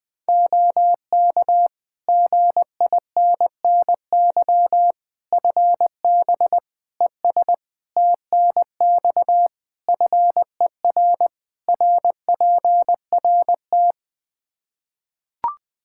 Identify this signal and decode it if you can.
Morse code, heard as a keyed tone, about 20 wpm, 700 Hz. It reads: OK GINNY FB ES TNX FER RPRT
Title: Practice QSO - Exchange 2 - Info - 20wpm